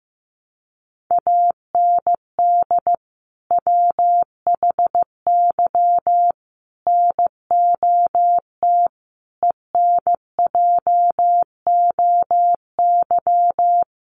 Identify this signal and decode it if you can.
Morse code, heard as a keyed tone, about 15 wpm, 700 Hz. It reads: AND WHY NOT ENJOY
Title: Sentences from Top 1300 Words 15wpm